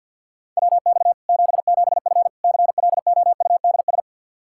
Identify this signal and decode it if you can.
Morse code, heard as a keyed tone, about 51 wpm, 700 Hz. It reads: WX 77F CLOUDS
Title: Practice QSO - Exchange 2 - Info - 50wpm